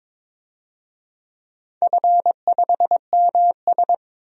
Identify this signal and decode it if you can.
Morse code, heard as a keyed tone, about 21 wpm, 700 Hz. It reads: F5MS
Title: DXCC-100 Call Signs - 4 Characters - 22wpm